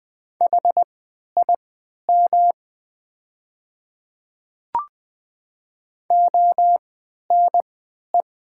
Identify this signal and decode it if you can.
Morse code, heard as a keyed tone, about 20 wpm, spaced as for 11 wpm, 700 Hz. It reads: HIM ONE
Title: 3x Character Spacing - Top 300 Words - 20wpm